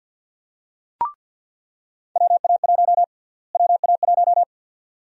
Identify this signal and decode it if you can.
Morse code, heard as a keyed tone, about 53 wpm, 700 Hz. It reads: WA1 WA1
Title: USA Call Sign Prefixes - Repeated 2x - 50wpm